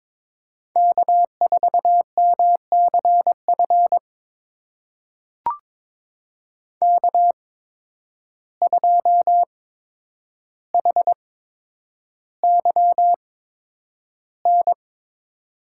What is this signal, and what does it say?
Morse code, heard as a keyed tone, about 22 wpm, spaced as for 9 wpm, 700 Hz. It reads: K4MCF K2HYN
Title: USA Call Signs - 5 Characters - 8x Character Spacing - 22wpm